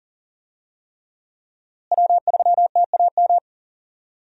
Morse code, heard as a keyed tone, about 40 wpm, 700 Hz. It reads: W3TAM
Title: USA Call Signs - 5 Characters - Repeated 3x - 40wpm